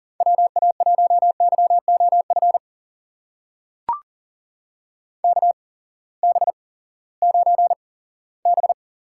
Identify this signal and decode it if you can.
Morse code, heard as a keyed tone, about 40 wpm, spaced as for 20 wpm, 700 Hz. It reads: WA1YOF KB9B
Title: USA Call Signs - 6 Characters - 8x Character Spacing - 40wpm